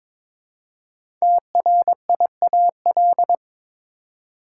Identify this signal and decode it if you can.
Morse code, heard as a keyed tone, about 22 wpm, 700 Hz. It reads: TRIAL
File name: 5 Letter Words - First 3 Letters Given - 22wpm